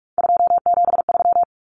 Morse code, heard as a keyed tone, about 44 wpm, 700 Hz. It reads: 273
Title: Common Cut Numbers - 3-Digit 45wpm